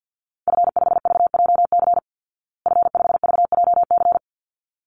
Morse code, heard as a keyed tone, about 49 wpm, 700 Hz. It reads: F5VJC F5VJC
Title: Call Signs Repeated 3x - 50wpm